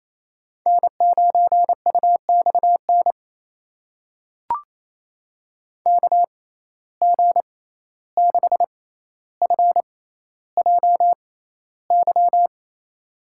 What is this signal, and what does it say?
Morse code, heard as a keyed tone, about 28 wpm, spaced as for 14 wpm, 700 Hz. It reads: N9UXN KG6FJY
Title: USA Call Signs - 6x Character Spacing - 28wpm